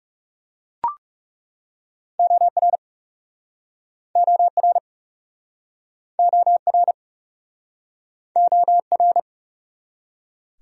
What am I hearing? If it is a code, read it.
Morse code, heard as a keyed tone, about 43 wpm, 700 Hz. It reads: OR OR OR OR